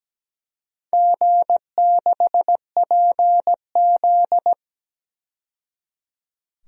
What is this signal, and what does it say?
Morse code, heard as a keyed tone, about 17 wpm, 700 Hz. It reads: G6PZ
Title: DXCC-100 Call Signs - 4 Characters - 17wpm